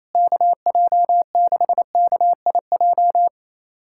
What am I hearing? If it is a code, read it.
Morse code, heard as a keyed tone, about 28 wpm, 700 Hz. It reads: KJ6KIJ
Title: USA Callsign DE Callsign - 6 Characters - 28wpm